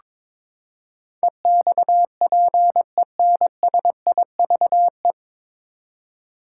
Morse code, heard as a keyed tone, about 22 wpm, 700 Hz. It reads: EXPENSIVE
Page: Morse Code Ninja practice 22wpm